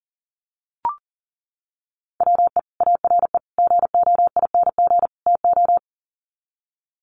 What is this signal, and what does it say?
Morse code, heard as a keyed tone, about 37 wpm, 700 Hz. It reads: WE ARE GOING TO
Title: Sets of 4 Words 40wpm